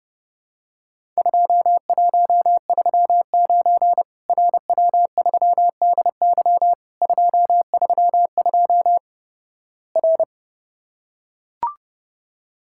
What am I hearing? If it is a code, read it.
Morse code, heard as a keyed tone, about 29 wpm, 700 Hz. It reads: 2139 RW3DY 232 R
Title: Worked All Europe DX Contest - QTC Exchange - 30wpm